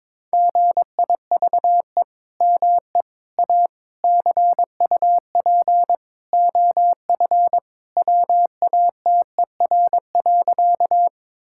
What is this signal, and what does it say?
Morse code, heard as a keyed tone, about 21 wpm, 700 Hz. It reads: GIVE ME A CUP OF WATER.
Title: Sentences from Top 1600 Words 22wpm